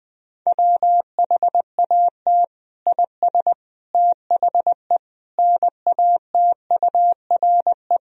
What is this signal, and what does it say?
Morse code, heard as a keyed tone, about 20 wpm, 700 Hz. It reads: WHAT IS THE NATURE
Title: Sentences from Top 1000 Words - Review Part 3 - 20wpm